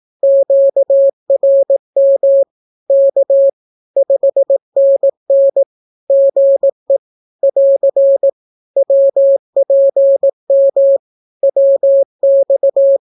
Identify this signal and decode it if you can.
Morse code, heard as a keyed tone, about 18 wpm, 550 Hz. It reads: QRM K 5NN GE <AR> WPM WX
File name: naukapodstaw / skroty / skroty